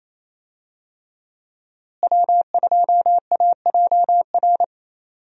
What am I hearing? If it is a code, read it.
Morse code, heard as a keyed tone, about 28 wpm, 700 Hz. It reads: W2AJR
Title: USA Call Signs - 5 Characters - Repeated 3x - 28wpm